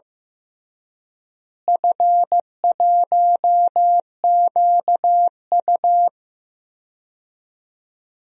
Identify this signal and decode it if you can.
Morse code, heard as a keyed tone, about 15 wpm, 700 Hz. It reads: F1QU
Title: DXCC-100 Call Signs - 4 Characters - Repeated 2x - 15wpm